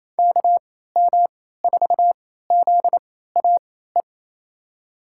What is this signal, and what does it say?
Morse code, heard as a keyed tone, about 28 wpm, spaced as for 16 wpm, 700 Hz. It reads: KM4ZAE
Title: USA Call Signs - 6 Characters - 3x Character Spacing - 28wpm